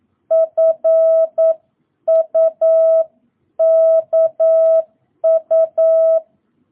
Morse code, heard as a keyed tone, about 9 wpm, 650 Hz. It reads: FUKU